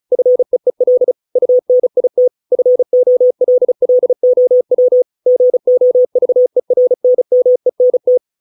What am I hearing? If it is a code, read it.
Morse code, heard as a keyed tone, about 34 wpm, 500 Hz. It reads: FEEL UNIT FOLLOW GOVERNMENT